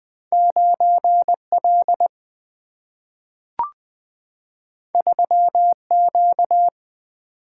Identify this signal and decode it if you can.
Morse code, heard as a keyed tone, about 20 wpm, 700 Hz. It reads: 9L 3Q
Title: Single-Letter Single-Number 20wpm